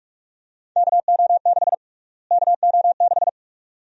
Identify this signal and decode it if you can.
Morse code, heard as a keyed tone, about 47 wpm, 700 Hz. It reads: KO6 KO6
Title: USA Call Sign Prefixes - Repeated 2x - 45wpm